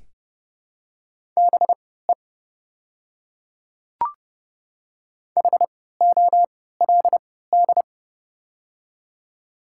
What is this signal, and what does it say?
Morse code, heard as a keyed tone, about 30 wpm, spaced as for 17 wpm, 700 Hz. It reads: BE HOLD
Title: 3x Character Spacing - Top 500 Words - 30wpm